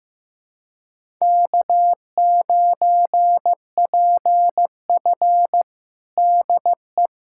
Morse code, heard as a keyed tone, about 15 wpm, 700 Hz. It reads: K9PF DE
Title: USA Callsign DE Callsign - 4 Characters - 15wpm